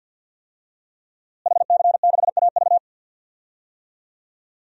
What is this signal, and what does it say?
Morse code, heard as a keyed tone, about 53 wpm, 700 Hz. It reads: HC6AV
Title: DXCC-100 Call Signs - 5 Characters - Repeated 2x - 50wpm